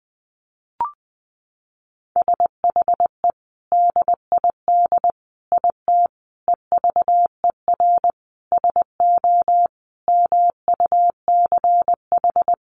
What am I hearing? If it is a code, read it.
Morse code, heard as a keyed tone, about 20 wpm, 700 Hz. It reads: SHE DID IT EVER SO MUCH
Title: Sentences from Top 400 Words 20wpm